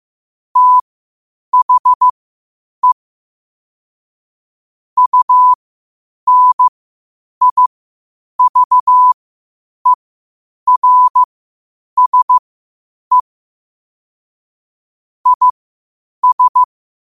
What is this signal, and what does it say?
Morse code, heard as a keyed tone, about 15 wpm, spaced as for 8 wpm, 1000 Hz. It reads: THE UNIVERSE IS
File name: qotdfiles / 8wpm